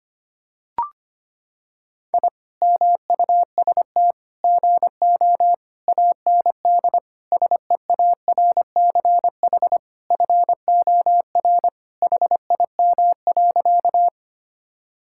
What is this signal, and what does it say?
Morse code, heard as a keyed tone, about 25 wpm, 700 Hz. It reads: I MUST GO AND SEARCH FOR HIM.
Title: Sentences from Top 1000 Words - Review Part 5 - 25wpm